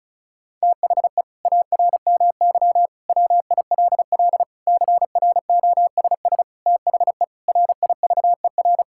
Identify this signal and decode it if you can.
Morse code, heard as a keyed tone, about 36 wpm, 700 Hz. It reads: THE ARMY WILL CROSS THE RIVER
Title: Sentences from Top 1000 Words - Review Part 5 - 35wpm